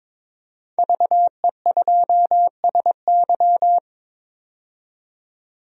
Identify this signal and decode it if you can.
Morse code, heard as a keyed tone, about 22 wpm, 700 Hz. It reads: VE2SY
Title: DXCC-100 Call Signs - 5 Characters - 22wpm